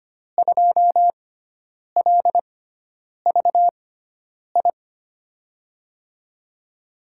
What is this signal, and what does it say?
Morse code, heard as a keyed tone, about 25 wpm, spaced as for 9 wpm, 700 Hz. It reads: 2LVI
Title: USA Call Signs - 6x Character Spacing - 25wpm